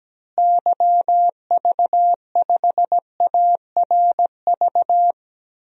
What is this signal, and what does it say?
Morse code, heard as a keyed tone, about 17 wpm, 700 Hz. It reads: YV5ARV
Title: DXCC-100 Call Signs - 6 Characters - Repeated 3x - 17wpm